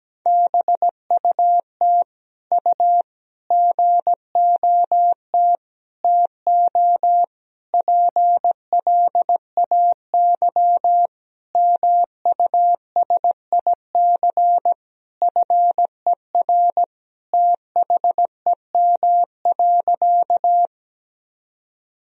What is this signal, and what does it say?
Morse code, heard as a keyed tone, about 17 wpm, 700 Hz. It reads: BUT U GOT TO PLAY MUSIC FER THEM.
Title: Sentences from Top 300 Words and Abbreviations 17wpm